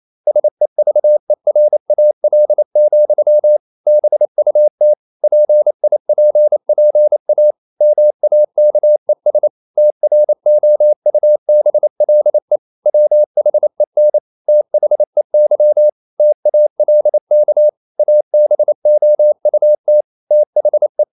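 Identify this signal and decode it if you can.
Morse code, heard as a keyed tone, about 28 wpm, 600 Hz. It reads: SEVERAL, BUT PIPPA MAKES TROUBLE WHEN THEY TALK ABOUT THE
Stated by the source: Morse code practice at 28 wpm, text of Ken Follett's novel Never in English, from CW med Gnister.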